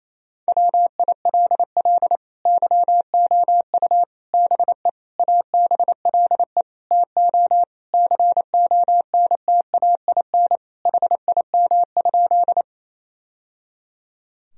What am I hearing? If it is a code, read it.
Morse code, heard as a keyed tone, about 28 wpm, 700 Hz. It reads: WILL YOU BE ABLE TO CONTAIN HIM?
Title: Sentences from Top 1000 Words - Review Part 2 - 28wpm